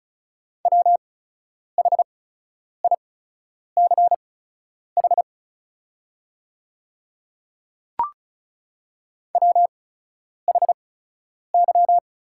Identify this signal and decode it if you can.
Morse code, heard as a keyed tone, about 34 wpm, spaced as for 10 wpm, 690 Hz. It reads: WHICH WHY
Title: 8x Character Spacing - Top 500 Words - 35wpm